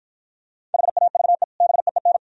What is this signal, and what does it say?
Morse code, heard as a keyed tone, about 49 wpm, 700 Hz. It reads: HAVE BEEN